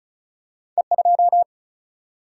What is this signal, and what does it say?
Morse code, heard as a keyed tone, about 36 wpm, 700 Hz. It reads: E2
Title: Single-Letter Single-Number 35wpm